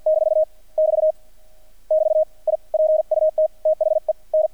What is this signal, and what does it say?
Morse code, heard as a keyed tone, about 48 wpm, 620 Hz. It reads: - <BT> <BT> I PUT THE N
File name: archived / CRSnet